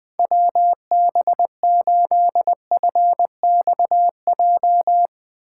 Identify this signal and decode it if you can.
Morse code, heard as a keyed tone, about 20 wpm, 700 Hz. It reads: WB8FXJ
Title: K1USN SST QSO Practice - 20wpm